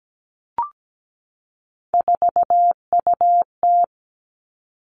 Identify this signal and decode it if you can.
Morse code, heard as a keyed tone, about 17 wpm, 700 Hz. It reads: 4UT
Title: Three Challenging Letter-Number Combinations 17wpm